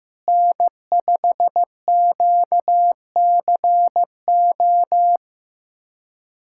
Morse code, heard as a keyed tone, about 15 wpm, 700 Hz. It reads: N5QCO
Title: USA Callsign DE Callsign - 5 Characters - 15wpm